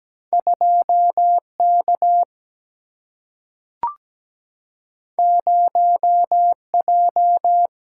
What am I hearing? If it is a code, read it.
Morse code, heard as a keyed tone, about 17 wpm, 700 Hz. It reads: 2 K 0 J
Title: Single-Letter Single-Number 17wpm